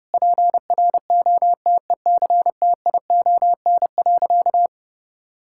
Morse code, heard as a keyed tone, about 30 wpm, 700 Hz. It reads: PROTECTION.
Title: Sentences from Top 1600 Words 30wpm